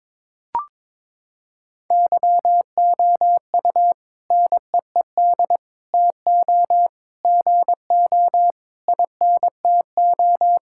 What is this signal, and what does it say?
Morse code, heard as a keyed tone, about 22 wpm, 700 Hz. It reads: YOU NEED TO GO INTO
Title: Sentences from Top 1000 Words - Review Part 6 - 22wpm